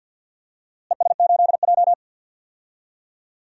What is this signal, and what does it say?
Morse code, heard as a keyed tone, about 52 wpm, 700 Hz. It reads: ES8J
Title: DXCC-100 Call Signs - 4 Characters - Repeated 2x - 50wpm